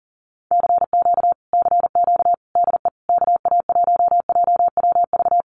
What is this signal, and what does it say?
Morse code, heard as a keyed tone, about 40 wpm, 700 Hz. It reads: CQ CQ DE KA1JWV
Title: Common QSO Phrases Encore 40wpm